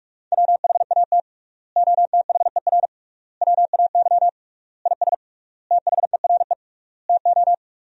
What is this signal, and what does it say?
Morse code, heard as a keyed tone, about 46 wpm, 700 Hz. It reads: WHAT OTHER WAY IS THERE TO
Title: 3x Word-Spacing - Sentences from Top 100 Words - 45wpm